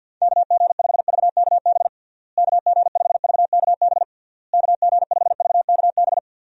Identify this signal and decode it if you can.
Morse code, heard as a keyed tone, about 53 wpm, 700 Hz. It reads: KG5VKB KG5VKB KG5VKB
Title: USA Call Signs - 6 Characters - Repeated 3x - 50wpm